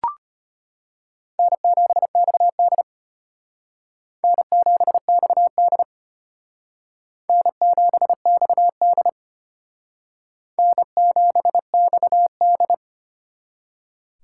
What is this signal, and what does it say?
Morse code, heard as a keyed tone, about 34 wpm, 700 Hz. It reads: N7XD N7XD N7XD N7XD